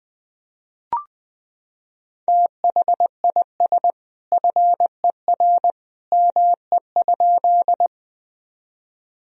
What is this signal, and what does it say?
Morse code, heard as a keyed tone, about 20 wpm, 700 Hz. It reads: THIS FER ME?
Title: Sentences from Top 200 Words and Abbreviations 20wpm